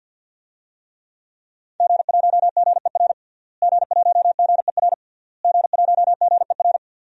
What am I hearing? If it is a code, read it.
Morse code, heard as a keyed tone, about 50 wpm, 700 Hz. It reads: G1GER G1GER G1GER
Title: DXCC-100 Call Signs - 5 Characters - Repeated 3x - 50wpm